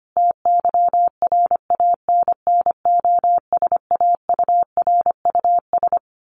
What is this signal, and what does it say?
Morse code, heard as a keyed tone, about 25 wpm, 700 Hz. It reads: TYRANNOSAURUS
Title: A Sound Of Thunder 25wpm